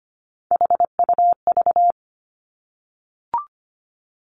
Morse code, heard as a keyed tone, about 25 wpm, 700 Hz. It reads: HUV
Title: Three Challenging Letter-Number Combinations 25wpm